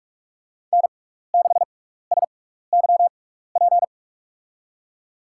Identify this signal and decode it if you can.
Morse code, heard as a keyed tone, about 47 wpm, spaced as for 16 wpm, 700 Hz. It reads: N6SYP
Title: USA Call Signs - 5 Characters - 6x Character Spacing - 45wpm